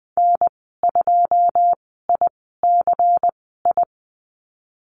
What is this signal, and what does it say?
Morse code, heard as a keyed tone, about 20 wpm, spaced as for 14 wpm, 700 Hz. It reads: N2ICI
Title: 128 Lesson 21g - Number 6 - Call Signs 20wpm